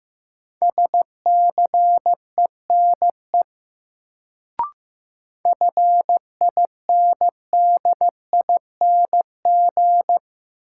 Morse code, heard as a keyed tone, about 15 wpm, 700 Hz. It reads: SCENE FINDING
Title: Sentences from Top 1100 Words - New Word Drill Alt - 15wpm